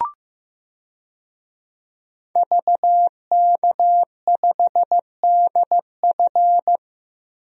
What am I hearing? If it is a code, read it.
Morse code, heard as a keyed tone, about 15 wpm, 700 Hz. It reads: VK5DF
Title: DXCC-100 Call Signs - 5 Characters - 15wpm